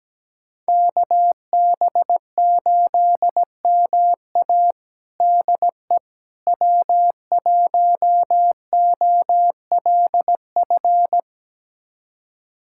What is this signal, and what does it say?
Morse code, heard as a keyed tone, about 17 wpm, 700 Hz. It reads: KB8MA DE W1OLF
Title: USA Callsign DE Callsign - 5 Characters - 17wpm